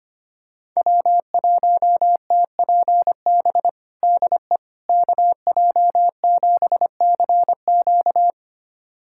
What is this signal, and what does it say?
Morse code, heard as a keyed tone, about 25 wpm, 700 Hz. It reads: W1TPB DE KJ7CQ
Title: USA Callsign DE Callsign - 5 Characters - 25wpm